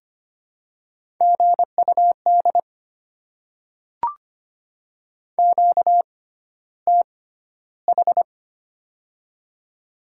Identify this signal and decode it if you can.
Morse code, heard as a keyed tone, about 25 wpm, spaced as for 13 wpm, 700 Hz. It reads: GUD QTH
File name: Top 20 QSO Elements - 6x Character Spacing - 25wpm